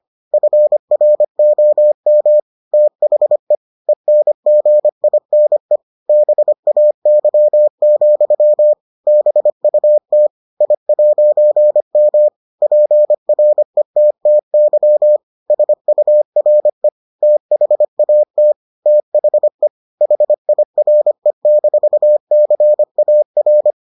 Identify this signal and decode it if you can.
Morse code, heard as a keyed tone, about 25 wpm, 600 Hz. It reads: FROM THE ENGINE BAY, BUT I'M PRETTY SURE THAT THE HIRE-CAR